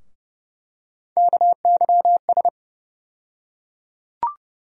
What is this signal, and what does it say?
Morse code, heard as a keyed tone, about 30 wpm, 700 Hz. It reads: KYS
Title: USA Call Sign Postfixes - 3 Letters 30wpm